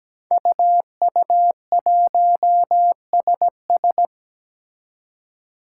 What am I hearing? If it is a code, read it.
Morse code, heard as a keyed tone, about 17 wpm, 700 Hz. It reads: UU1SS
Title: Five Challenging Letter-Number Combinations 17wpm